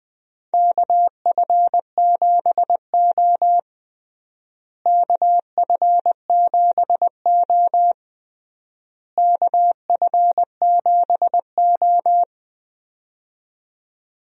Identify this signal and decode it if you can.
Morse code, heard as a keyed tone, about 21 wpm, 700 Hz. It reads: KF7O KF7O KF7O
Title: USA Call Signs - 4 Characters - Repeated 3x - 20wpm